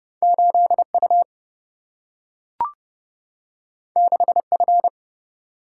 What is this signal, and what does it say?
Morse code, heard as a keyed tone, about 30 wpm, 700 Hz. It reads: 8U 6F
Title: Single-Letter Single-Number 30wpm